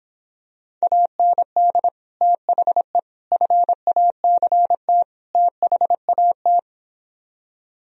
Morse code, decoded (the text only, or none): AND THE FACT THAT